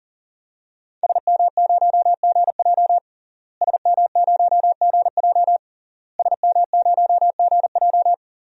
Morse code, heard as a keyed tone, about 40 wpm, 700 Hz. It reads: SM0GJ SM0GJ SM0GJ
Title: DXCC-100 Call Signs - 5 Characters - Repeated 3x - 40wpm